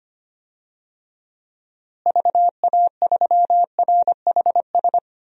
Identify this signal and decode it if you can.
Morse code, heard as a keyed tone, about 25 wpm, 700 Hz. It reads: VA3RHS
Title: DXCC-100 Call Signs - 6 Characters - Repeated 2x - 25wpm